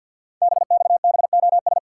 Morse code, heard as a keyed tone, about 48 wpm, 700 Hz. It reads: BXBOS
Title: Code Groups - 5 Letters - 50wpm